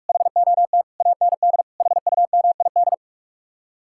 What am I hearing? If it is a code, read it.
Morse code, heard as a keyed tone, about 49 wpm, 700 Hz. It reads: HOT AND HUMID